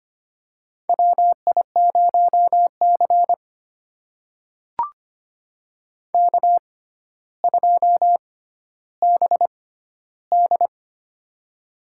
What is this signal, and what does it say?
Morse code, heard as a keyed tone, about 25 wpm, spaced as for 13 wpm, 700 Hz. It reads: WI0C K2BD
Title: USA Call Signs - 4 Characters - 6x Character Spacing - 25wpm